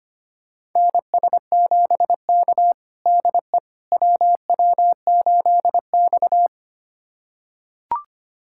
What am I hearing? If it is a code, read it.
Morse code, heard as a keyed tone, about 25 wpm, 700 Hz. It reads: NS7K DE WW8X
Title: USA Callsign DE Callsign - 4 Characters - 25wpm